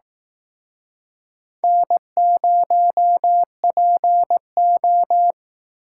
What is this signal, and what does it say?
Morse code, heard as a keyed tone, about 18 wpm, 700 Hz. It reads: N0PO
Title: Speed Racing - USA 1x2 Call Signs 15wpm